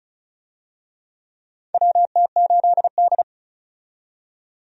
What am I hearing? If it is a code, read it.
Morse code, heard as a keyed tone, about 36 wpm, 700 Hz. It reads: WT8D
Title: USA Call Signs - 4 Characters - 35wpm